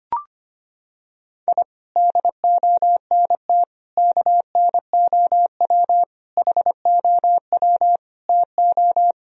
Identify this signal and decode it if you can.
Morse code, heard as a keyed tone, about 25 wpm, 700 Hz. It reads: I DONT KNOW HOW TO
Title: Sentences from Top 1000 Words - Review Part 1 - 25wpm